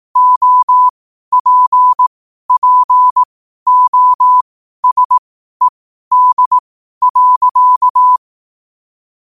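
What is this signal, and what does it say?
Morse code, heard as a keyed tone, about 18 wpm, spaced as for 12 wpm, 1000 Hz. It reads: OPPOSED.
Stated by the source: Morse code quote of the day at 12 wpm.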